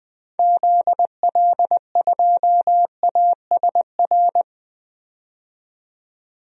Morse code, heard as a keyed tone, about 20 wpm, 700 Hz. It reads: ZL2ASR
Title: DXCC-100 Call Signs - 6 Characters - 20wpm